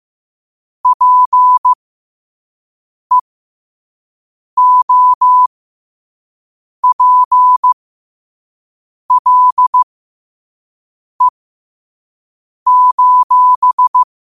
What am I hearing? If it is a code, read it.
Morse code, heard as a keyed tone, about 15 wpm, spaced as for 5 wpm, 1000 Hz. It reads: PEOPLE:
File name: qotdfiles / 5wpm